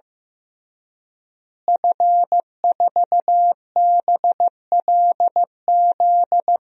Morse code, heard as a keyed tone, about 15 wpm, 700 Hz. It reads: F4BLZ
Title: DXCC-100 Call Signs - 5 Characters - Repeated 3x - 15wpm